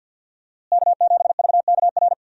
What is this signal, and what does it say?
Morse code, heard as a keyed tone, about 50 wpm, 700 Hz. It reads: K7VKR
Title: USA Call Signs - 5 Characters - Repeated 2x - 50wpm